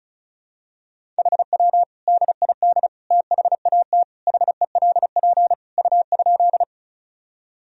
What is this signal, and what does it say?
Morse code, heard as a keyed tone, about 36 wpm, 700 Hz. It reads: HW DID THAT HELP U?
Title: Sentences from Top 200 Words and Abbreviations 35wpm